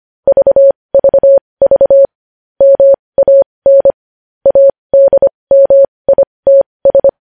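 Morse code, heard as a keyed tone, about 25 wpm, spaced as for 20 wpm, 550 Hz. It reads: VVV MAN ADMITS
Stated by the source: Morse practice sending news headlines at 20 wpm.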